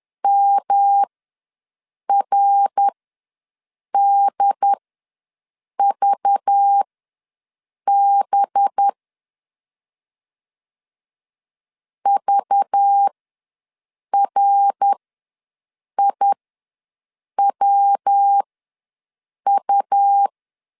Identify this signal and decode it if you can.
Morse code, heard as a keyed tone, about 10 wpm, spaced as for 6 wpm, 800 Hz. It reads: MRDVB VRIWU
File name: LEZIONI / 5 LBVC / 5 LBV ws 10wpm 3645